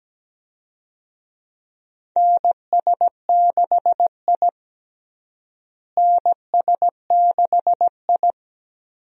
Morse code, heard as a keyed tone, about 17 wpm, 700 Hz. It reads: NS6I NS6I
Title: USA Call Signs - 4 Characters - Repeated 2x - 17wpm